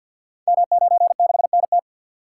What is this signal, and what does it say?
Morse code, heard as a keyed tone, about 48 wpm, 700 Hz. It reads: M96NT
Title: Code Groups - 5 Letter:Numbers - 50wpm